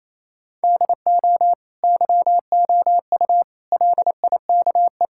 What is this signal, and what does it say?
Morse code, heard as a keyed tone, about 28 wpm, 700 Hz. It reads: DO YOU LIKE
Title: Sentences from Top 1600 Words 28wpm